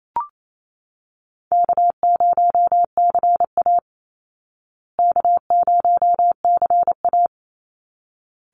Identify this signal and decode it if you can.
Morse code, heard as a keyed tone, about 28 wpm, 700 Hz. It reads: K0CA K0CA
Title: Easy Call Signs Repeated 2x - 28wpm